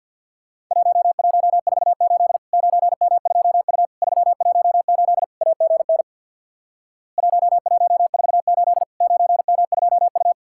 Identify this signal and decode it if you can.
Morse code, heard as a keyed tone, about 50 wpm, 700 Hz. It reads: 1148 9M2U 318 AGN 1148 9M2U
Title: Worked All Europe DX Contest - QTC Exchange with Repeats - 50wpm